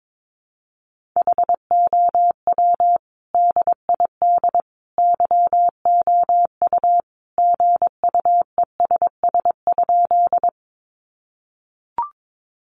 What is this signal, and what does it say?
Morse code, heard as a keyed tone, about 22 wpm, 700 Hz. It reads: HOW DID YOU GUESS?
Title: Sentences from Top 1500 Words 22wpm